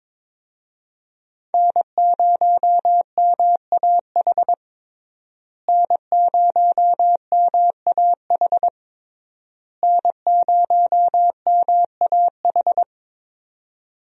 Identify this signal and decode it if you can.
Morse code, heard as a keyed tone, about 22 wpm, 700 Hz. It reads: N0MAH N0MAH N0MAH
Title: USA Call Signs - 5 Characters - Repeated 3x - 22wpm